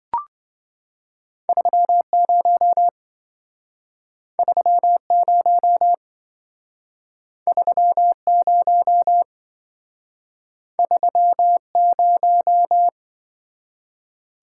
Morse code, heard as a keyed tone, about 25 wpm, 700 Hz. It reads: 30 30 30 30
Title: Speed-Racing - Two Letter-Number Combinations - 20wpm